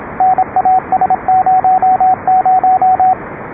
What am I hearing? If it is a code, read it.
Morse code, heard as a keyed tone, about 26 wpm, 780 Hz, with radio noise. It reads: NAS00